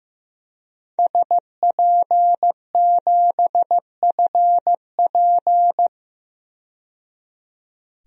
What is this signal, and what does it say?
Morse code, heard as a keyed tone, about 15 wpm, 700 Hz. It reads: SP7FP
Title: DXCC-100 Call Signs - 5 Characters - 15wpm